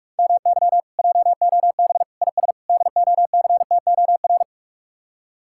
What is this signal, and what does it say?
Morse code, heard as a keyed tone, about 46 wpm, 700 Hz. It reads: MY JOB IS DOCTOR